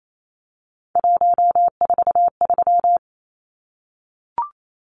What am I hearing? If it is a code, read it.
Morse code, heard as a keyed tone, about 28 wpm, 700 Hz. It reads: 143
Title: Numbers - 3-Digit 28wpm